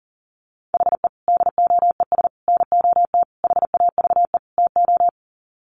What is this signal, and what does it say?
Morse code, heard as a keyed tone, about 40 wpm, 700 Hz. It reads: HE DOES NOT HAVE TO
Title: Sets of 5 Words Encore 40wpm